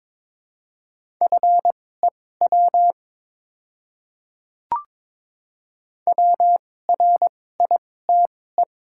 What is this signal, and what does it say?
Morse code, heard as a keyed tone, about 22 wpm, spaced as for 16 wpm, 700 Hz. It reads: FEW WRITE
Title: 2x Character Spacing - Top 300 Words - 22wpm